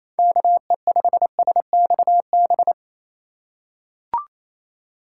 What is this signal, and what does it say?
Morse code, heard as a keyed tone, about 28 wpm, 700 Hz. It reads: KE5SXB
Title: USA Call Signs - 6 Characters - 4x Character Spacing - 28wpm